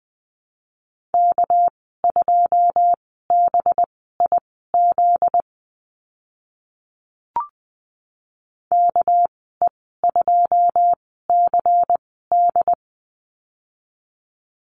Morse code, heard as a keyed tone, about 20 wpm, spaced as for 14 wpm, 700 Hz. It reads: K2BIZ KE2CD
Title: 245 Lesson 39g - Letter z - Call Signs 20wpm